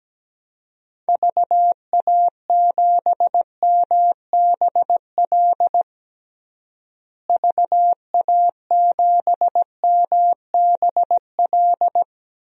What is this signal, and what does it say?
Morse code, heard as a keyed tone, about 17 wpm, 700 Hz. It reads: VA7MBL VA7MBL
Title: DXCC-100 Call Signs - 6 Characters - Repeated 2x - 17wpm